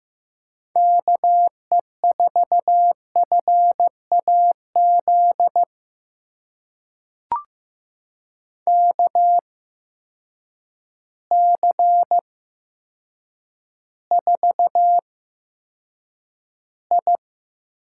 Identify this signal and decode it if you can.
Morse code, heard as a keyed tone, about 15 wpm, spaced as for 7 wpm, 700 Hz. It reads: KE4FAZ KC4I
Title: USA Call Signs - 8x Character Spacing - 15wpm